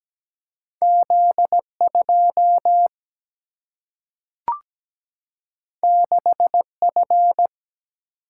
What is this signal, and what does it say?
Morse code, heard as a keyed tone, about 17 wpm, 700 Hz. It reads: Z2 6F
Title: Single-Letter Single-Number 17wpm